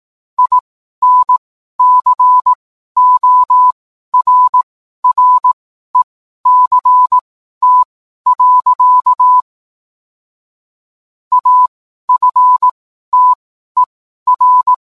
Morse code, heard as a keyed tone, about 18 wpm, spaced as for 12 wpm, 1000 Hz. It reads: INCORRECT. AFTER